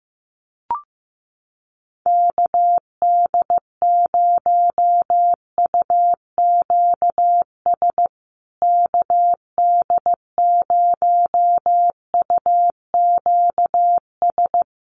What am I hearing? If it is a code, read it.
Morse code, heard as a keyed tone, about 15 wpm, 700 Hz. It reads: KD0UQS KD0UQS
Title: USA Call Signs Repeated 3x 15wpm